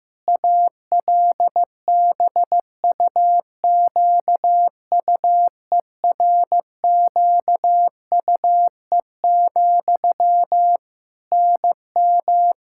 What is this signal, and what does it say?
Morse code, heard as a keyed tone, about 15 wpm, 700 Hz. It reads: ALBUQUERQUE, NM
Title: K1USN SST QSO Practice 15wpm